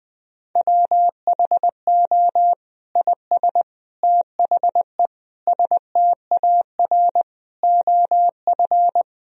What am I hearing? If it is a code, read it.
Morse code, heard as a keyed tone, about 21 wpm, 700 Hz. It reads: WHO IS THE STAR OF
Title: Sentences from Top 1000 Words - Review Part 2 - 20wpm